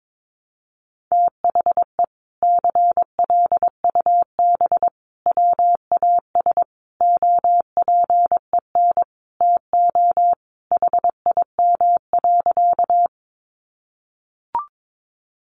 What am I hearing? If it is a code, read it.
Morse code, heard as a keyed tone, about 22 wpm, 700 Hz. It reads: THE CLUB WAS OPEN TO HIM.
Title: Sentences from Top 1200 Words 22wpm